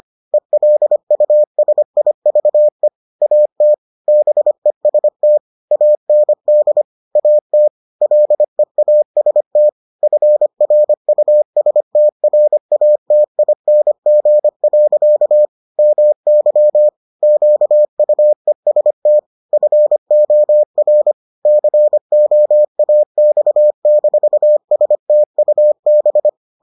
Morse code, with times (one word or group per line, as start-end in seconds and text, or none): 0.3-2.9 ELUSIVE
3.2-3.7 AT
4.1-5.4 BEST
5.7-6.8 AND
7.2-7.7 AT
8.0-9.7 LEAST
10.0-15.5 FRUSTRATING.
15.8-16.9 MY
17.2-19.2 QUEST
19.5-21.1 FOR
21.5-26.3 COAX-STUB